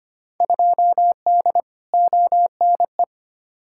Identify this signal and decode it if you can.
Morse code, heard as a keyed tone, about 25 wpm, 700 Hz. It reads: 2D ONE
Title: Field Day Exchanges - 25wpm